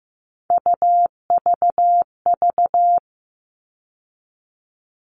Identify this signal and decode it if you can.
Morse code, heard as a keyed tone, about 15 wpm, 700 Hz. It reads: UVV
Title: All Cut Numbers - 3-Digit 15wpm